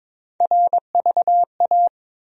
Morse code, heard as a keyed tone, about 22 wpm, 700 Hz. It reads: RVA
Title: USA Call Sign Postfixes - 3 Letters 22wpm